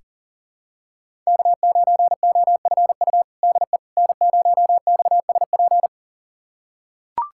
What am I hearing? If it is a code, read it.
Morse code, heard as a keyed tone, about 40 wpm, 700 Hz. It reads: K9OFU DE N0XSP
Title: USA Callsign DE Callsign - 5 Characters - 40wpm